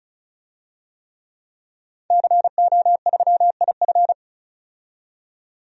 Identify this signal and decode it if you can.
Morse code, heard as a keyed tone, about 36 wpm, 700 Hz. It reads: CO3IF
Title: DXCC-100 Call Signs - 5 Characters - 35wpm